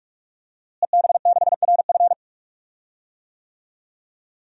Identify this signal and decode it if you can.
Morse code, heard as a keyed tone, about 46 wpm, 700 Hz. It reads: EB6RF
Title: DXCC-100 Call Signs - 5 Characters - Repeated 3x - 45wpm